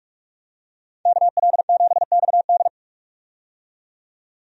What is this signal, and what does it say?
Morse code, heard as a keyed tone, about 45 wpm, 700 Hz. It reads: KL7XD
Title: DXCC-100 Call Signs - 5 Characters - 45wpm